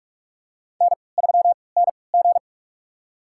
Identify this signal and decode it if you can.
Morse code, heard as a keyed tone, about 42 wpm, spaced as for 25 wpm, 700 Hz. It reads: N3NG
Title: USA Call Signs - 4 Characters - 3x Character Spacing - 45wpm